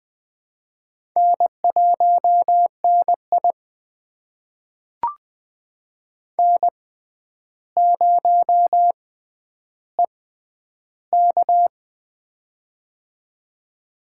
Morse code, heard as a keyed tone, about 20 wpm, spaced as for 10 wpm, 700 Hz. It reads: N1NI N0EK
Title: USA Call Signs - 4 Characters - 6x Character Spacing - 20wpm